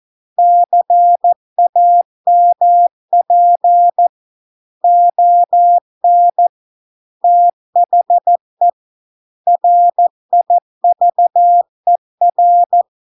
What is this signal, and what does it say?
Morse code, heard as a keyed tone, about 14 wpm, 700 Hz. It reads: CAMP ON THE RIVER